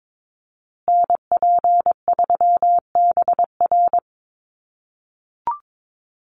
Morse code, heard as a keyed tone, about 22 wpm, 700 Hz. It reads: NP3BR
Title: USA Call Signs Repeated 2x 22wpm